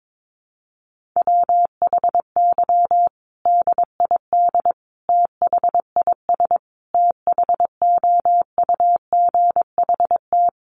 Morse code, heard as a keyed tone, about 22 wpm, 700 Hz. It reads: WHY DID THIS THOUGHT
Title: Sentences from Top 900 Words 22wpm